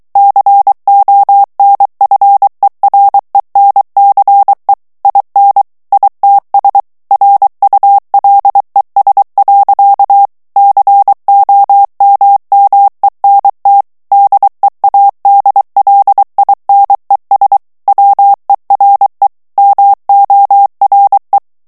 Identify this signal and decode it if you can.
Morse code, heard as a keyed tone, about 23 wpm, 800 Hz. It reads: CONFERENCE IN ITS RULES. COMMENT DEADLINES WERE MORE
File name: ANGLAIS texte clair 3 vitesse 22 mots minute